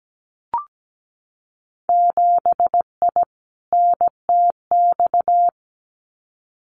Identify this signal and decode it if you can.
Morse code, heard as a keyed tone, about 17 wpm, 700 Hz. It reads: 7I NTX
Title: Winter Field Day Exchanges Encore 17wpm